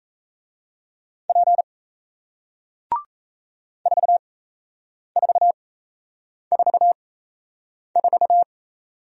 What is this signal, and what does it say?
Morse code, heard as a keyed tone, about 37 wpm, spaced as for 8 wpm, 700 Hz. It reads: P 4444